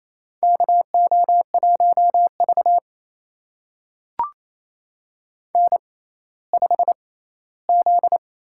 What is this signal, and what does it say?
Morse code, heard as a keyed tone, about 29 wpm, spaced as for 16 wpm, 700 Hz. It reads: KO1V N5Z
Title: USA Call Signs - 4 Characters - 6x Character Spacing - 28wpm